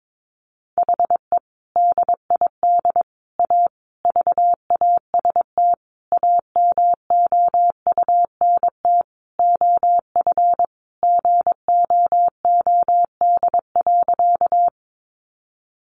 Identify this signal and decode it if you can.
Morse code, heard as a keyed tone, about 22 wpm, 700 Hz. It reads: HE DID A VAST AMOUNT OF GOOD.
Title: Sentences from Top 1300 Words 22wpm